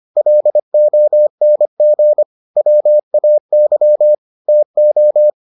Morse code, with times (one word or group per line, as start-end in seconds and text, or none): 0.2-2.2 LONG
2.6-4.1 WAY
4.5-5.3 TO